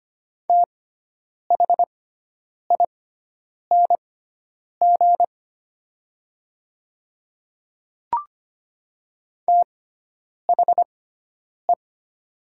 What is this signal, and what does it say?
Morse code, heard as a keyed tone, about 25 wpm, spaced as for 9 wpm, 700 Hz. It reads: THING THE
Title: 6x Character Spacing - Top 100 Words - 25wpm